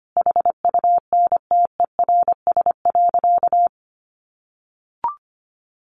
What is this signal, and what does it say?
Morse code, heard as a keyed tone, about 25 wpm, 700 Hz. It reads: HUNTERS.
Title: A Sound Of Thunder 25wpm